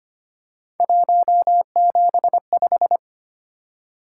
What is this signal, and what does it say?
Morse code, heard as a keyed tone, about 25 wpm, 700 Hz. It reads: 175
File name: Three Letter-Number Combinations - 25wpm